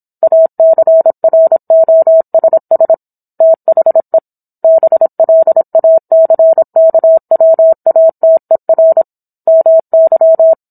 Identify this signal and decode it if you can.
Morse code, heard as a keyed tone, about 27 wpm, 650 Hz. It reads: ACROSS THE BLACKWATER MY
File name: Book One / 17-Chapter 17 - 26 WPM